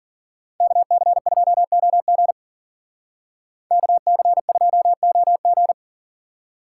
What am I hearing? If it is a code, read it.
Morse code, heard as a keyed tone, about 48 wpm, 700 Hz. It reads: KC2OG KC2OG
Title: Speed Racing - USA 2x2 Call Signs 40wpm